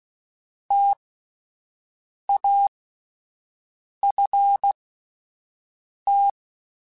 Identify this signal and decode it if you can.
Morse code, heard as a keyed tone, about 16 wpm, spaced as for 5 wpm, 800 Hz. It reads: TAFT